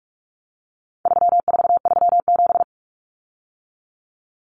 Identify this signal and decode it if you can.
Morse code, heard as a keyed tone, about 44 wpm, 700 Hz. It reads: 3437
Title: Common Cut Numbers - 4-Digit 45wpm